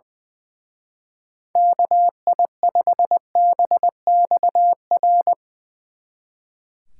Morse code, heard as a keyed tone, about 21 wpm, 700 Hz. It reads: KI5BXR
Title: Speed Racing - USA 2x3 Call Signs 20wpm